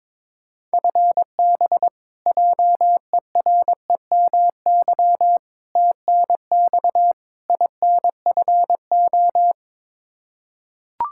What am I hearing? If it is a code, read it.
Morse code, heard as a keyed tone, about 21 wpm, 700 Hz. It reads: FB JEREMY TNX INFO
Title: Practice QSO - Exchange 3 - More Info - 22wpm